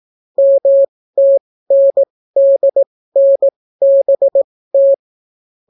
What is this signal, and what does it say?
Morse code, heard as a keyed tone, about 18 wpm, spaced as for 14 wpm, 550 Hz. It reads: MTNDNBT